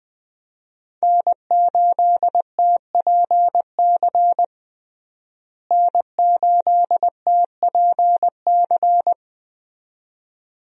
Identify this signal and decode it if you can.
Morse code, heard as a keyed tone, about 20 wpm, 700 Hz. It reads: N8TPC N8TPC
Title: USA Call Signs - 5 Characters - Repeated 2x - 20wpm